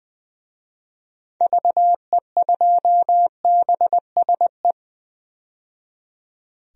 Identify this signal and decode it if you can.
Morse code, heard as a keyed tone, about 21 wpm, 700 Hz. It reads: VE2BSE
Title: DXCC-100 Call Signs - 6 Characters - 20wpm